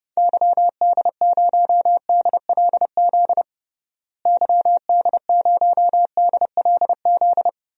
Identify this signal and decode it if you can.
Morse code, heard as a keyed tone, about 32 wpm, 700 Hz. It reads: YD0DLZ YD0DLZ
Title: DXCC-100 Call Signs - 6 Characters - Repeated 3x - 30wpm